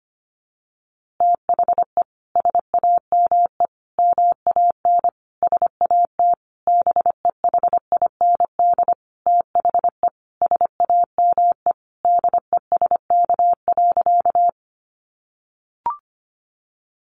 Morse code, heard as a keyed tone, about 25 wpm, 700 Hz. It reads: THE SAME MAN SAT BEHIND THE SAME DESK.
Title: A Sound Of Thunder 25wpm